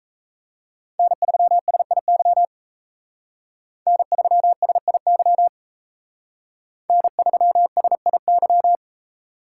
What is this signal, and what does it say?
Morse code, heard as a keyed tone, about 35 wpm, 700 Hz. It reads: N3SIY N3SIY N3SIY
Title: Speed Racing - USA 1x3 Call Signs 28wpm